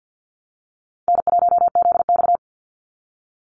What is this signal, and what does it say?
Morse code, heard as a keyed tone, about 50 wpm, 700 Hz. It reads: N1ZX
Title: Easy Call Signs Repeated 3x - 50wpm